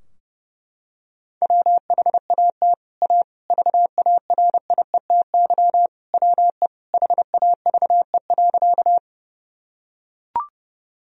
Morse code, heard as a keyed tone, about 30 wpm, 700 Hz. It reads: WHAT A VARIETY WE HAVE.
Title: Sentences from Top 1000 Words - Review Part 4 - 30wpm